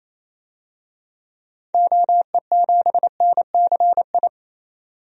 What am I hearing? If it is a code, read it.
Morse code, heard as a keyed tone, about 28 wpm, 700 Hz. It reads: OE7NCI
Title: DXCC-100 Call Signs - 6 Characters - 28wpm